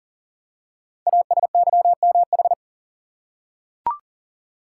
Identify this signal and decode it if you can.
Morse code, heard as a keyed tone, about 40 wpm, 700 Hz. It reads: ASYMH
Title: Code Groups - 5 Letters - 40wpm